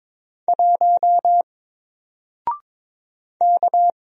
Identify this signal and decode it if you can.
Morse code, heard as a keyed tone, about 22 wpm, 700 Hz. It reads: 1 K
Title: Single Character (Letter:Number:Punctuation:Prosign) Alt 22wpm